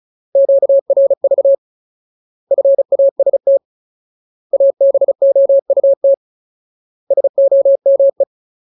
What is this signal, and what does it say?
Morse code, heard as a keyed tone, about 34 wpm, 550 Hz. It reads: QRV FAST ABOUT SOME
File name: Ultimate Word List / 35WPM Ultimate